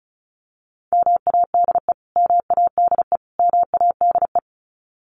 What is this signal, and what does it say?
Morse code, heard as a keyed tone, about 34 wpm, 700 Hz. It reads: MADE MADE MADE
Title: Top 300 Words Repeated 35wpm